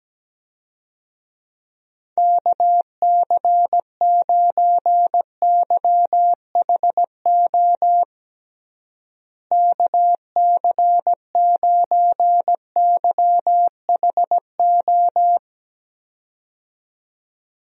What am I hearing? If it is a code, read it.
Morse code, heard as a keyed tone, about 17 wpm, 700 Hz. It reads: KC9YHO KC9YHO
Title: USA Call Signs - 6 Characters - Repeated 2x - 17wpm